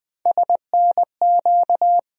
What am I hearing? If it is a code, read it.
Morse code, heard as a keyed tone, about 20 wpm, 700 Hz. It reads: SNQ
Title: USA Call Sign Postfixes - 3 Letters 20wpm